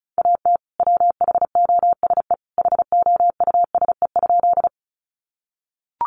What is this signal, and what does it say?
Morse code, heard as a keyed tone, about 34 wpm, 700 Hz. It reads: AT WHOSE HOUSE?
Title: Sentences from Top 1200 Words 35wpm